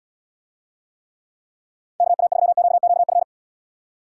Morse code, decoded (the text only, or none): KA3QQF